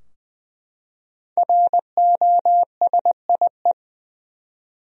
Morse code, heard as a keyed tone, about 20 wpm, 700 Hz. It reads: ROSIE
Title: USA Common Names - 20wpm